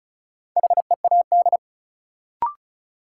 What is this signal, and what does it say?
Morse code, heard as a keyed tone, about 34 wpm, 700 Hz. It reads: HEAD
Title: Sentences from Top 300 Words - New Word Drill - 35wpm